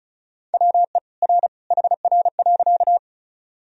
Morse code, heard as a keyed tone, about 35 wpm, 700 Hz. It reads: WE R HR.
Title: Sentences from Top 200 Words and Abbreviations 35wpm